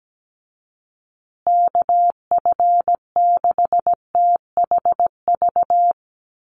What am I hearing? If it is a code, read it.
Morse code, heard as a keyed tone, about 17 wpm, 700 Hz. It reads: KF6THV
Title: USA Call Signs Repeated 2x 17wpm